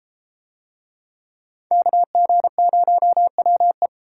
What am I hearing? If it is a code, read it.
Morse code, heard as a keyed tone, about 31 wpm, 700 Hz. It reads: KG0WE